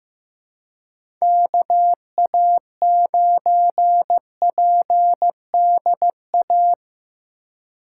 Morse code, heard as a keyed tone, about 15 wpm, 700 Hz. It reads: KA9PDA
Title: USA Call Signs - 6 Characters - 3x Character Spacing - 15wpm